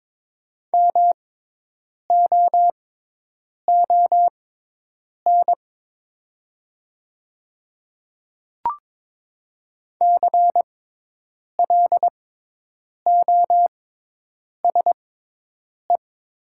Morse code, heard as a keyed tone, about 22 wpm, spaced as for 8 wpm, 700 Hz. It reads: MOON CLOSE
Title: 6x Character Spacing - Top 500 Words - 22wpm